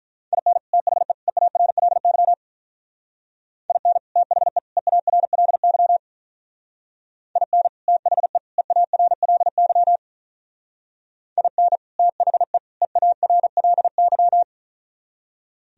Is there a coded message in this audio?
Yes: Morse code from 0.3 to 14.4 s, heard as a keyed tone, about 44 wpm, 700 Hz. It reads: IN THE EARLY IN THE EARLY IN THE EARLY IN THE EARLY